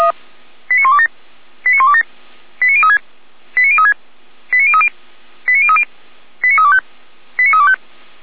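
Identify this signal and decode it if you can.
Five-tone selective call (ZVEI), heard as quick runs of five tones, tones of about 70 ms. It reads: 89127 89127 89036 89036 89030 89030 89325 89325
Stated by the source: Received by SDR as ZVEI five-tone selective call.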